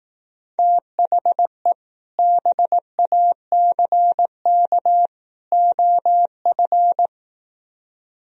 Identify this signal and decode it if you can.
Morse code, heard as a keyed tone, about 18 wpm, 700 Hz. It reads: THE BACK OF
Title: Speed-Racing - Sets of 3 Words 15wpm